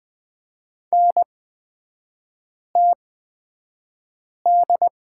Morse code, heard as a keyed tone, about 20 wpm, spaced as for 5 wpm, 700 Hz. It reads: NTD
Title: 224 Lesson 36e - BT - Three Characters 20wpm